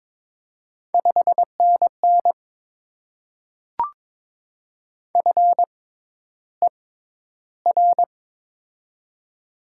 Morse code, heard as a keyed tone, about 22 wpm, spaced as for 11 wpm, 700 Hz. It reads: 5NN FER
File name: Top 20 QSO Elements - 6x Character Spacing - 22wpm